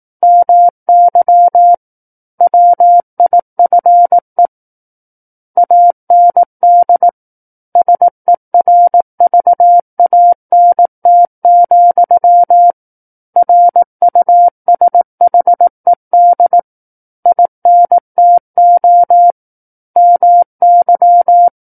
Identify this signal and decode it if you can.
Morse code, heard as a keyed tone, about 18 wpm, 700 Hz. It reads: MY WIFE AND SERVANT, RUSHED INTO MY
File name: 18 WPM